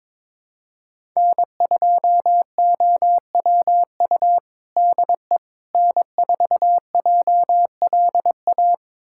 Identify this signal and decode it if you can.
Morse code, heard as a keyed tone, about 21 wpm, 700 Hz. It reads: N2OWU DE N4JLA
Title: USA Callsign DE Callsign - 5 Characters - 22wpm